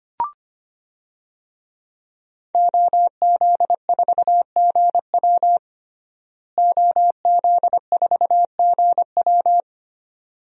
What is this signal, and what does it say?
Morse code, heard as a keyed tone, about 25 wpm, 700 Hz. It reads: OZ4GW OZ4GW
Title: DXCC-100 Call Signs - 5 Characters - Repeated 2x - 25wpm